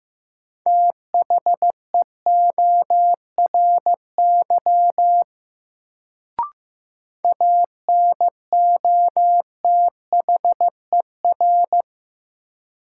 Morse code, heard as a keyed tone, about 15 wpm, 700 Hz. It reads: THEORY ANOTHER
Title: Sentences from Top 600 Words - New Word Drill - 15wpm